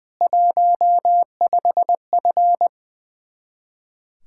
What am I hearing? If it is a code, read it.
Morse code, heard as a keyed tone, about 21 wpm, 700 Hz. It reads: 15F
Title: Speed-Racing Common QSO Phrases - 20wpm